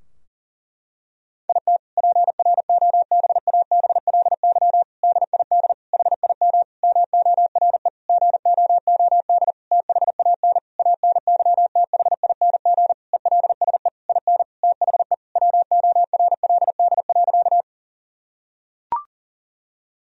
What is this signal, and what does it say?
Morse code, heard as a keyed tone, about 40 wpm, 700 Hz. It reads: IT PROBABLY DID HIM MORE GOOD THAN ANYTHING ELSE IN THE WORLD.
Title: Sentences from Top 1000 Words - Review Part 4 - 40wpm